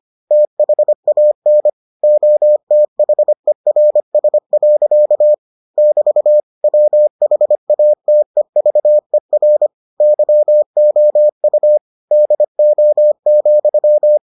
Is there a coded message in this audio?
Yes: Morse code from 0.3 to 14.2 s, heard as a keyed tone, about 25 wpm, 600 Hz. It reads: THAN OTHERS. <BT> WHATEVER YOU DO,